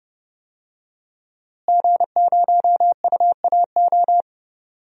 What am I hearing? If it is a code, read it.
Morse code, heard as a keyed tone, about 29 wpm, 700 Hz. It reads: G0UAO
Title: DXCC-100 Call Signs - 5 Characters - Repeated 2x - 30wpm